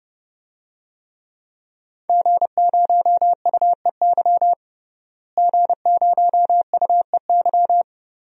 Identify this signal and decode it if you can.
Morse code, heard as a keyed tone, about 32 wpm, 700 Hz. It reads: G0UEY G0UEY
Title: DXCC-100 Call Signs - 5 Characters - Repeated 2x - 30wpm